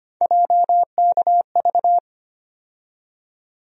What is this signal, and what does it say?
Morse code, heard as a keyed tone, about 25 wpm, 700 Hz. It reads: JKV
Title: USA Call Sign Postfixes - 3 Letters 25wpm